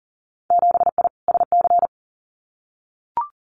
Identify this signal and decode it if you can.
Morse code, heard as a keyed tone, about 42 wpm, 700 Hz. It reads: 7I SC
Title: Winter Field Day Exchanges Encore 40wpm